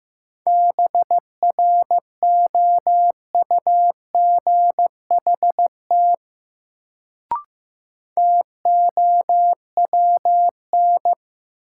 Morse code, heard as a keyed tone, about 15 wpm, 700 Hz. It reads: BROUGHT TOWN
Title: Sentences from Top 500 Words - New Word Drill - 15wpm